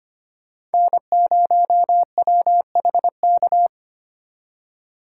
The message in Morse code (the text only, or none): N0WHK